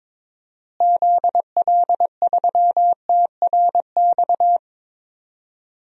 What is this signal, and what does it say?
Morse code, heard as a keyed tone, about 21 wpm, 700 Hz. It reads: ZL3TRX
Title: DXCC-100 Call Signs - 6 Characters - Repeated 3x - 22wpm